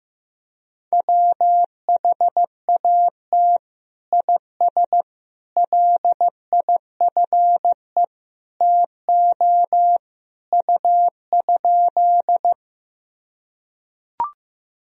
Morse code, heard as a keyed tone, about 15 wpm, 700 Hz. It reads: WHAT IS LIFE TO U?
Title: Sentences from Top 300 Words and Abbreviations 15wpm